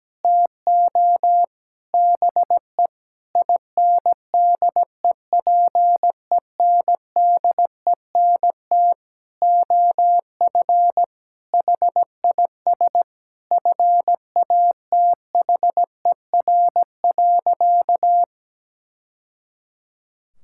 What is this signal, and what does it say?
Morse code, heard as a keyed tone, about 17 wpm, 700 Hz. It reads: TO BE INDEPENDENT OF HIS FATHER.
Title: Sentences from Top 1000 Words - Review Part 6 - 17wpm